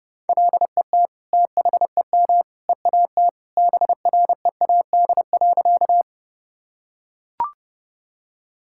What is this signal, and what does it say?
Morse code, heard as a keyed tone, about 30 wpm, 700 Hz. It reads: LET THEM EAT BREAD.
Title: Sentences from Top 1600 Words 30wpm